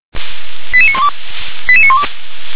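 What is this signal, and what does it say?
Five-tone selective call (ZVEI), heard as quick runs of five tones, tones of about 70 ms. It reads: 88912 88912
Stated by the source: ZVEI five-tone selective call received on an SDR.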